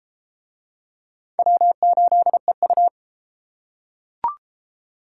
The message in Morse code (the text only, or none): W8EU